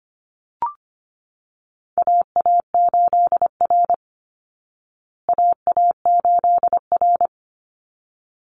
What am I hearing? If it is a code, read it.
Morse code, heard as a keyed tone, about 25 wpm, 700 Hz. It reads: AA8R AA8R
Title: Call Signs Repeated 3x - 25wpm